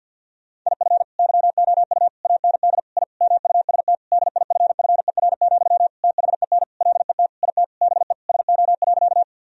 Morse code, heard as a keyed tone, about 52 wpm, 700 Hz. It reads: IF YOU AND I MUST DIFFER, THEN LET IT BE SO.
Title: Sentences from Top 1000 Words - Review Part 1 - 50wpm